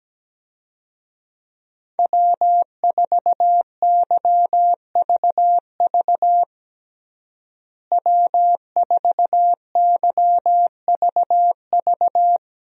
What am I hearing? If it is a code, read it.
Morse code, heard as a keyed tone, about 17 wpm, 700 Hz. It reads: W4YVV W4YVV
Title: USA Call Signs - 5 Characters - Repeated 2x - 17wpm